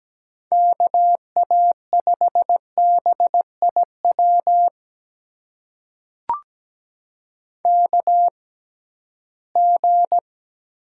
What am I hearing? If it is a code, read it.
Morse code, heard as a keyed tone, about 17 wpm, spaced as for 13 wpm, 700 Hz. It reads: KA5BIW KG
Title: USA Call Signs - 6 Characters - 6x Character Spacing - 17wpm